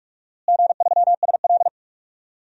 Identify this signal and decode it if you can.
Morse code, heard as a keyed tone, about 46 wpm, 700 Hz. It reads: G3SL
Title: DXCC-100 Call Signs - 4 Characters - Repeated 3x - 45wpm